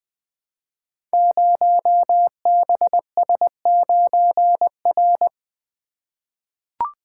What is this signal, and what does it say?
Morse code, heard as a keyed tone, about 20 wpm, 700 Hz. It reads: 0BS9R
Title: Code Groups - 5 Letter:Numbers - 20wpm